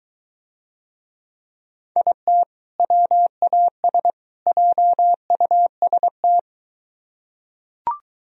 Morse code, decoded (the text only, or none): IT WAS JUST